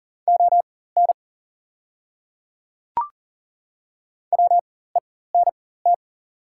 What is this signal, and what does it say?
Morse code, heard as a keyed tone, about 38 wpm, spaced as for 19 wpm, 700 Hz. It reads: ON WENT
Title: 4x Character Spacing - Top 200 Words - 40wpm